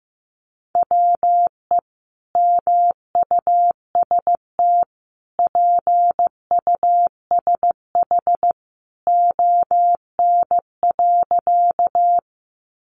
Morse code, heard as a keyed tone, about 15 wpm, 700 Hz. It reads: WE MUST PUSH ON.
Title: Sentences from Top 1300 Words 15wpm